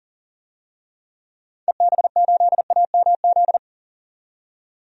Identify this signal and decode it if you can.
Morse code, heard as a keyed tone, about 40 wpm, 700 Hz. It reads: EB8AMZ
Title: DXCC-100 Call Signs - 6 Characters - 40wpm